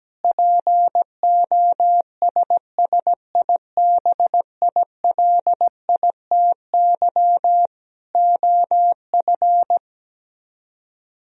Morse code, heard as a keyed tone, about 17 wpm, 700 Hz. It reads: POSSIBILITY OF
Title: Speed-Racing - Sets of 3 Words 17wpm